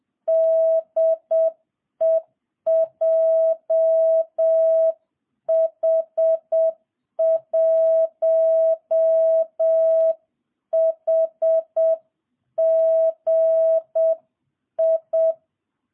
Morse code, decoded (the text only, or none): DE JH1HGI